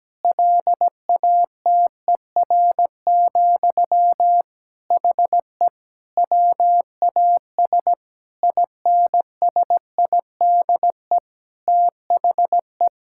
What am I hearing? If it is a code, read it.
Morse code, heard as a keyed tone, about 17 wpm, 700 Hz. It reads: LATER, HE WAS INSIDE THE
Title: Sentences from Top 1000 Words - Review Part 6 - 17wpm